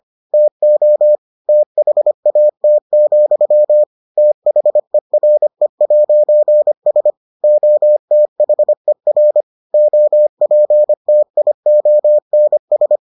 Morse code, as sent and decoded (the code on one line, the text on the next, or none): - --- / - .... .- - --..-- / - .... .-. .----. ... / --- - .... .-. / --- .--. - .. --- -. ...
TO THAT, THERE'S OTHER OPTIONS